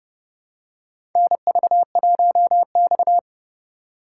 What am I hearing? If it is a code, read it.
Morse code, heard as a keyed tone, about 29 wpm, 700 Hz. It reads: NV1X
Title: USA Call Signs - 4 Characters - 2x Character Spacing - 30wpm